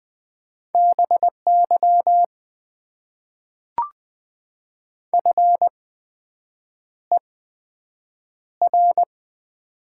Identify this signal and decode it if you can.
Morse code, heard as a keyed tone, about 20 wpm, spaced as for 7 wpm, 700 Hz. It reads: BY FER
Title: Top 90 QSO Elements - 8x Character Spacing - 20wpm